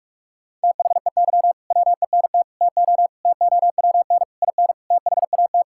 Morse code, heard as a keyed tone, about 42 wpm, 700 Hz. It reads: THEY WENT TO TOWN IN THAT
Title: Sentences from Top 1000 Words - Review Part 5 - 45wpm